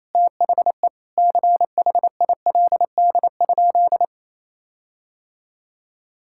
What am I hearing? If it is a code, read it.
Morse code, heard as a keyed tone, about 28 wpm, 700 Hz. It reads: THE CHILD?
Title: Sentences from Top 1600 Words 28wpm